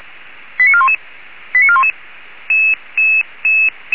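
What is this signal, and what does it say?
Five-tone selective call (ZVEI), heard as quick runs of five tones, tones of about 70 ms. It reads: 89411 89411
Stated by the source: ZVEI five-tone selective call received on an SDR.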